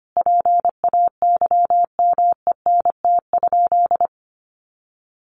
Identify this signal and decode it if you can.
Morse code, heard as a keyed tone, about 25 wpm, 700 Hz. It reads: PAYMENT?
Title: Sentences from Top 1400 Words 25wpm